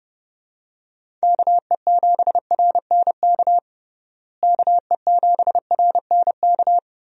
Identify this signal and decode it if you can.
Morse code, heard as a keyed tone, about 30 wpm, 700 Hz. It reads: KE7RNK KE7RNK
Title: USA Call Signs - 6 Characters - Repeated 2x - 30wpm